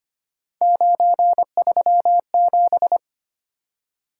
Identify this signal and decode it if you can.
Morse code, heard as a keyed tone, about 25 wpm, 700 Hz. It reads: 937
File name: Three Letter-Number Combinations - 25wpm